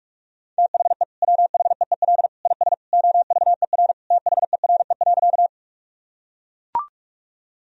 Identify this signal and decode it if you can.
Morse code, heard as a keyed tone, about 42 wpm, 700 Hz. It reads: THE WHEEL IS OVER THERE.
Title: Sentences from Top 1000 Words - Review Part 3 - 45wpm